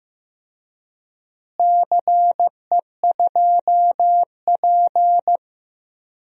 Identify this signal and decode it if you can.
Morse code, heard as a keyed tone, about 15 wpm, 700 Hz. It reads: CE2P
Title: DXCC-100 Call Signs - 4 Characters - Repeated 3x - 15wpm